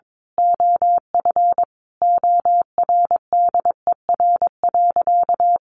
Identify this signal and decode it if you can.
Morse code, heard as a keyed tone, about 22 wpm, 700 Hz. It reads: OF ORDER.
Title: Sentences from Top 1400 Words 22wpm